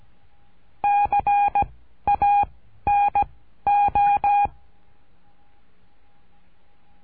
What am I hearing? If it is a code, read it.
Morse code, heard as a keyed tone, about 17 wpm, 820 Hz. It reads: C A N O
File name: cursos / CURSO DE TELEGRAFIA / Telegrafia - Trilha 13